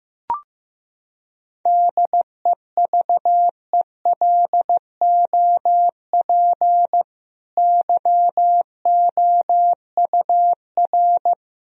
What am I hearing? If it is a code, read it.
Morse code, heard as a keyed tone, about 15 wpm, 700 Hz. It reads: DEVELOP YOUR
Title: Sentences from Top 1000 Words - Review Part 2 - 15wpm